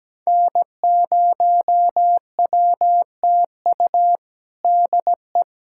Morse code, heard as a keyed tone, about 17 wpm, 700 Hz. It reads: N0WTU DE
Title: USA Callsign DE Callsign - 5 Characters - 17wpm